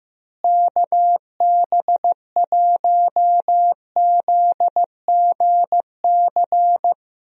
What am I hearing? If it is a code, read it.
Morse code, heard as a keyed tone, about 15 wpm, 700 Hz. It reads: KB1ZGC
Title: USA Call Signs - 6 Characters - Repeated 2x - 15wpm